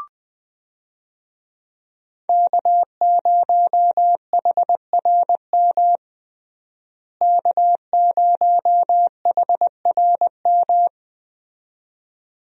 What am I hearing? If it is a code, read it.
Morse code, heard as a keyed tone, about 21 wpm, 700 Hz. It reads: K0HRM K0HRM
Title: USA Call Signs - 5 Characters - Repeated 2x - 20wpm